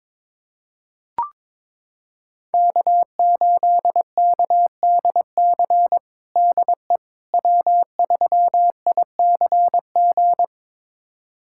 Morse code, heard as a keyed tone, about 21 wpm, 700 Hz. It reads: K8KDC DE W3ICG
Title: USA Callsign DE Callsign - 5 Characters - 22wpm